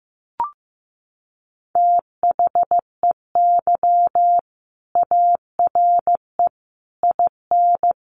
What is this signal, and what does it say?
Morse code, heard as a keyed tone, about 15 wpm, 700 Hz. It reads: THEY ARE IN
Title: Sentences from Top 600 Words 15wpm